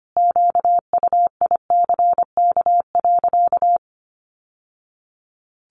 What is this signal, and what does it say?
Morse code, heard as a keyed tone, about 25 wpm, 700 Hz. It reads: QUICK.
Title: Sentences from Top 600 Words 25wpm